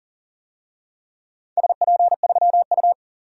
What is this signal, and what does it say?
Morse code, heard as a keyed tone, about 40 wpm, 700 Hz. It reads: SP3U
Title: DXCC-100 Call Signs - 4 Characters - 40wpm